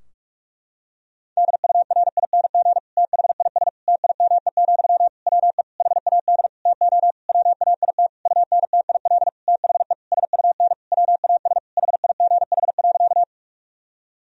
Morse code, heard as a keyed tone, about 46 wpm, 700 Hz. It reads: DURING THIS TIME, WE HAD TO WAIT UNTIL THE SUN WAS HIGH.
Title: Sentences from Top 1000 Words - Review Part 2 - 45wpm